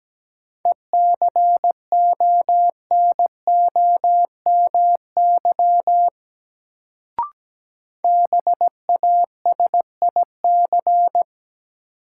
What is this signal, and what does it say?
Morse code, heard as a keyed tone, about 17 wpm, 700 Hz. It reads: ECONOMY BASIC
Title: Sentences from Top 1000 Words - Review Part 3 - New Word Drill - 17wpm